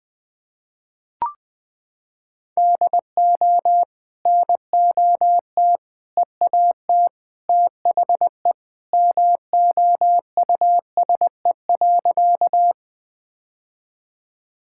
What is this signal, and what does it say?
Morse code, heard as a keyed tone, about 20 wpm, 700 Hz. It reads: DO NOT EAT THE MOUSE.
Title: Sentences from Top 1000 Words - Review Part 5 - 20wpm